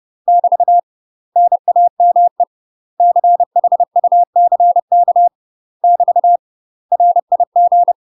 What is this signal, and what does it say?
Morse code, heard as a keyed tone, about 30 wpm, 700 Hz. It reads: <BT> NAME CHUCK <BT> RIG
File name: UK MASTER QSOs TRAINING / 30WPM / UK MASTER QSOs 0014